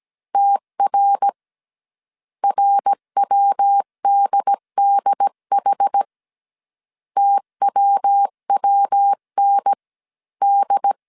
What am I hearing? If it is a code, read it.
Morse code, heard as a keyed tone, about 17 wpm, 800 Hz. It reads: TR RWDDH TWWN D